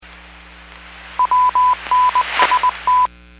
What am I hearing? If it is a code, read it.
Morse code, heard as a keyed tone, about 20 wpm, 1000 Hz. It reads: WNST